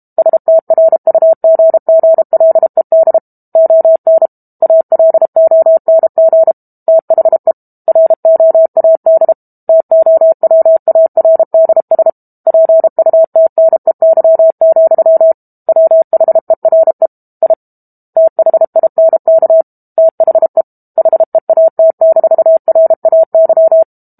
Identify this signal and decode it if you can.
Morse code, heard as a keyed tone, about 32 wpm, 650 Hz. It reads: STRUGGLED ON ALONG THE ROAD TOWARDS PUTNEY, WHERE I THINK THE HEAT-RAY
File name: Book Two / 06-Chapter 6 - 32 WPM